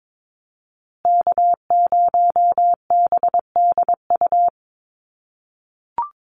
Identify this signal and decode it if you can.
Morse code, heard as a keyed tone, about 22 wpm, 700 Hz. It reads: K0BDU
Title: USA Call Signs Repeated 3x 22wpm